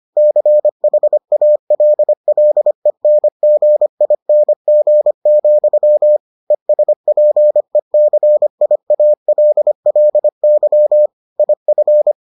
Morse code, heard as a keyed tone, about 25 wpm, 600 Hz. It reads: CHALLENGING, ESPECIALLY IF